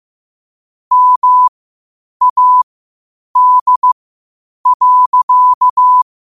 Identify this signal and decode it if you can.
Morse code, heard as a keyed tone, about 15 wpm, spaced as for 8 wpm, 1000 Hz. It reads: MAD.
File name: qotdfiles / 8wpm